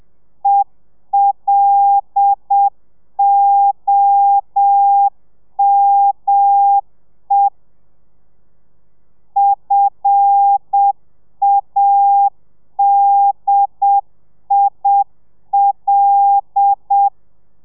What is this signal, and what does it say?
Morse code, heard as a keyed tone, about 7 wpm, 800 Hz. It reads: ELOME FADIL
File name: telegrafia / i2tfj / Lez5 30